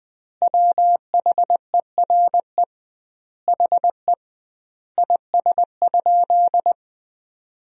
Morse code, heard as a keyed tone, about 20 wpm, 700 Hz. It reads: WHERE HE IS?
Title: 2x Word-Spacing - Sentences from Top 300 Words - 20wpm